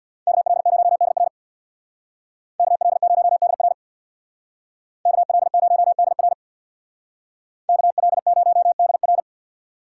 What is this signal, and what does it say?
Morse code, heard as a keyed tone, about 63 wpm, 700 Hz. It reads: KL0DR KL0DR KL0DR KL0DR